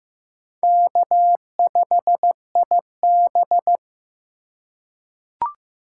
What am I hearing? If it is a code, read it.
Morse code, heard as a keyed tone, about 15 wpm, 700 Hz. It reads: K5IB
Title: USA Call Signs - 4 Characters - 2x Character Spacing - 15wpm